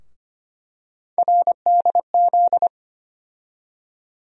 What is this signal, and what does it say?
Morse code, heard as a keyed tone, about 25 wpm, 700 Hz. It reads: RDZ
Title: USA Call Sign Postfixes - 3 Letters 25wpm